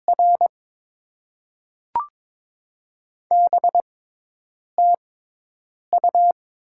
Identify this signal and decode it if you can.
Morse code, heard as a keyed tone, about 22 wpm, spaced as for 8 wpm, 700 Hz. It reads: R BTU
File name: Top 20 QSO Elements - 6x Character Spacing - 22wpm